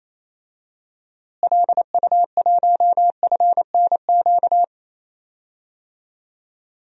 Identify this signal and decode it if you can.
Morse code, heard as a keyed tone, about 28 wpm, 700 Hz. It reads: LU1FNQ
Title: DXCC-100 Call Signs - 6 Characters - Repeated 2x - 28wpm